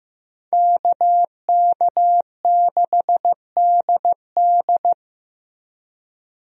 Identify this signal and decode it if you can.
Morse code, heard as a keyed tone, about 15 wpm, 700 Hz. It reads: KK6DD
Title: USA Call Signs - 5 Characters - Repeated 2x - 15wpm